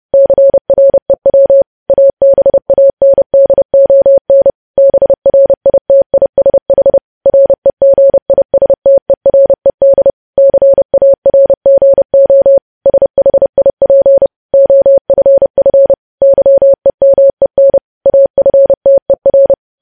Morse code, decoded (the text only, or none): CREW ABANDON BRITISH REGISTERED CARGO SHIP OFF YEMEN AFTER